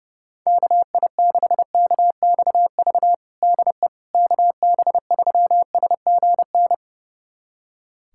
Morse code, heard as a keyed tone, about 32 wpm, 700 Hz. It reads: KI6KXV DE KB3SGN
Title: USA Callsign DE Callsign 30wpm